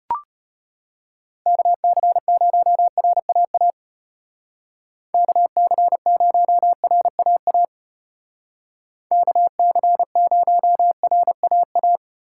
Morse code, heard as a keyed tone, about 33 wpm, 700 Hz. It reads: KC0RAA KC0RAA KC0RAA